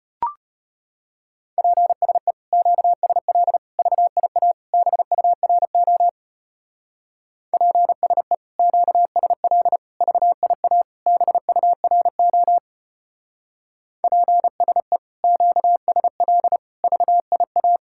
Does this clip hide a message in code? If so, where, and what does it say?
Yes: Morse code from 1.6 to 17.8 s, heard as a keyed tone, about 32 wpm, 700 Hz. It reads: PSE QSL VIA BURO PSE QSL VIA BURO PSE QSL VIA